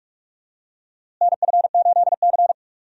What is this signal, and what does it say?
Morse code, heard as a keyed tone, about 47 wpm, 700 Hz. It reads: NF8C
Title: USA Call Signs - 4 Characters - Repeated 3x - 45wpm